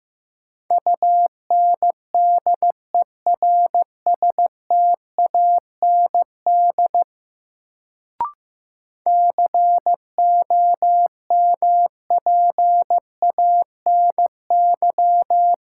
Morse code, heard as a keyed tone, about 15 wpm, 700 Hz. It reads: UNDERSTAND COMPANY
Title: Sentences from Top 600 Words - New Word Drill - 15wpm